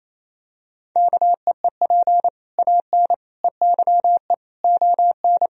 Morse code, heard as a keyed tone, about 28 wpm, 700 Hz. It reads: KEEP AN EYE ON
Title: Sentences from Top 1000 Words - Review Part 1 - 28wpm